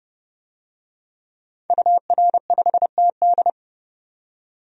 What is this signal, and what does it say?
Morse code, heard as a keyed tone, about 30 wpm, 700 Hz. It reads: UR5TD
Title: DXCC-100 Call Signs - 5 Characters - 30wpm